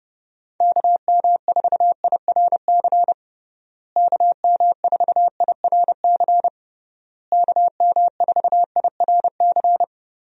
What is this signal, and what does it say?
Morse code, heard as a keyed tone, about 32 wpm, 700 Hz. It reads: KM4IRC KM4IRC KM4IRC
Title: USA Call Signs - 6 Characters - Repeated 3x - 30wpm